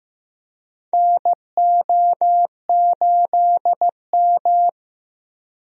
Morse code, heard as a keyed tone, about 15 wpm, 700 Hz. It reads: NO8M
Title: USA Call Signs - 4 Characters - Repeated 2x - 15wpm